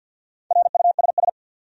Morse code, heard as a keyed tone, about 52 wpm, 700 Hz. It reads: RUSS
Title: USA Common Names - Max 4 Letters - 50wpm